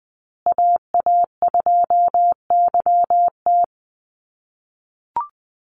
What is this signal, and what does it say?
Morse code, heard as a keyed tone, about 20 wpm, 700 Hz. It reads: AA2YT
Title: USA Call Signs Repeated 3x 20wpm